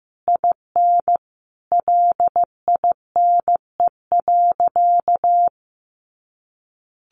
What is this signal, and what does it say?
Morse code, heard as a keyed tone, about 15 wpm, 700 Hz. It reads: IN LINE.
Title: Additional Easy Phrases 15wpm